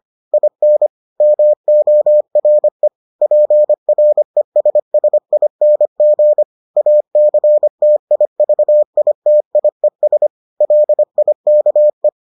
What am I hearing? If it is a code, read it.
Morse code, heard as a keyed tone, about 25 wpm, 600 Hz. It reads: IN MORE PRESSING ACTIVITIES LIKE